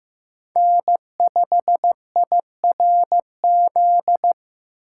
Morse code, heard as a keyed tone, about 15 wpm, 700 Hz. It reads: N5IRZ
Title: USA Call Signs - 5 Characters - 4x Character Spacing - 15wpm